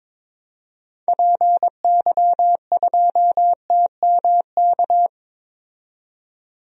Morse code, heard as a keyed tone, about 22 wpm, 700 Hz. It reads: PY2TMK
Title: DXCC-100 Call Signs - 6 Characters - 22wpm